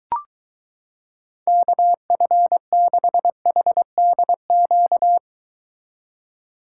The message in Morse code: KF6HDQ

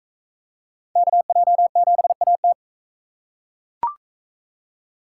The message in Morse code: KJ7AT

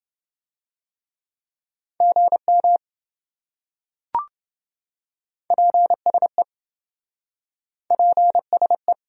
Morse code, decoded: GM PSE PSE

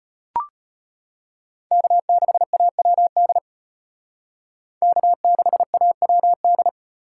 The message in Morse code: K6AWD K6AWD

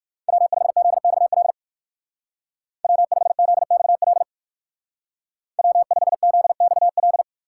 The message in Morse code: W5ZXL W5ZXL W5ZXL

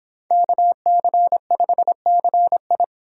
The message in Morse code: KC5CI

